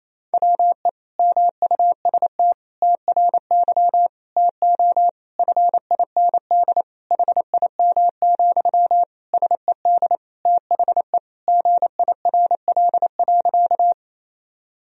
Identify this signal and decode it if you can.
Morse code, heard as a keyed tone, about 27 wpm, 700 Hz. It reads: WE MUST TRY TO FIND HIM, SED THE GIRL.
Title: Sentences from Top 400 Words and Abbreviations 28wpm